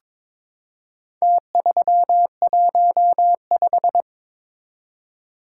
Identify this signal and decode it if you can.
Morse code, heard as a keyed tone, about 22 wpm, 700 Hz. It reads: T315
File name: practice / 22wpm / Four Letter-Number Combinations - 22wpm